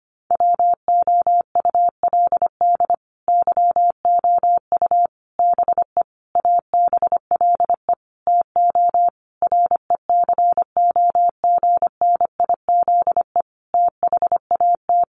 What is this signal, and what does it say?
Morse code, heard as a keyed tone, about 25 wpm, 700 Hz. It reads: WOULD YOU BE ABLE TO RECOGNIZE THAT
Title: Sentences from Top 1400 Words 25wpm